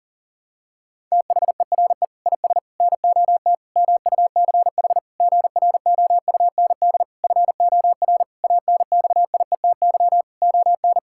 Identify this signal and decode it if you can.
Morse code, heard as a keyed tone, about 37 wpm, 700 Hz. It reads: THERE IS NOT MUCH GROUND FOR ANXIETY ON
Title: Sentences from Top 1600 Words 40wpm